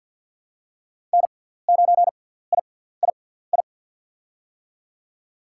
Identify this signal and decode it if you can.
Morse code, heard as a keyed tone, about 47 wpm, spaced as for 17 wpm, 700 Hz. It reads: N9III
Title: USA Call Signs - 5 Characters - 6x Character Spacing - 50wpm